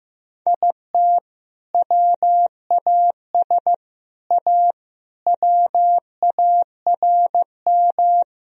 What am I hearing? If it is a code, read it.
Morse code, heard as a keyed tone, about 15 wpm, 700 Hz. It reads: IT WAS A WARM
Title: Sentences from Top 1000 Words - Review Part 2 - 15wpm